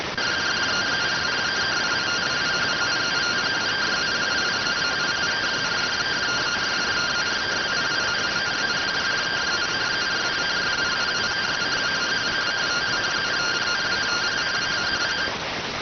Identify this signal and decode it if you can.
Two-tone teleprinter signal, about 170 Hz shift.